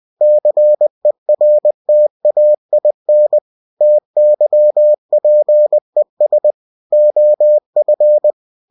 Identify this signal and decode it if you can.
Morse code, heard as a keyed tone, about 20 wpm, 600 Hz. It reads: CERTAIN TYPES OF
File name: cw-books / winnie-the-pooh / winnie-20wpm-page-0119